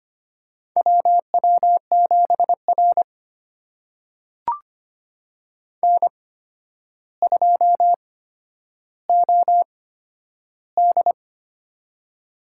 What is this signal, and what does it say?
Morse code, heard as a keyed tone, about 25 wpm, spaced as for 11 wpm, 700 Hz. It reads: WW7R N2OD
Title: USA Call Signs - 4 Characters - 8x Character Spacing - 25wpm